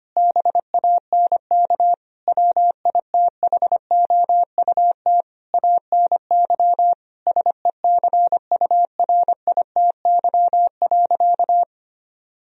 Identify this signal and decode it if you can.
Morse code, heard as a keyed tone, about 25 wpm, 700 Hz. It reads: BANK WITHOUT ANY SECURITY.
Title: Sentences from Top 1000 Words - Review Part 4 - 25wpm